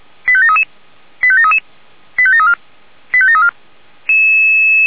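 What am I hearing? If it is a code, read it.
Five-tone selective call (ZVEI), heard as quick runs of five tones, tones of about 70 ms. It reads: 85733 85733 85734 85734
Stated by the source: ZVEI five-tone selective call received on an SDR.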